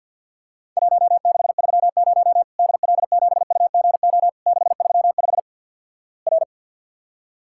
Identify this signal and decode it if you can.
Morse code, heard as a keyed tone, about 50 wpm, 700 Hz. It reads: 1630 DL7UGO 635 R
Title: Worked All Europe DX Contest - QTC Exchange - 50wpm